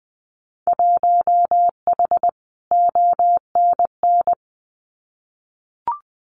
Morse code, heard as a keyed tone, about 20 wpm, 700 Hz. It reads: 1H ONN
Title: Winter Field Day Exchanges 20wpm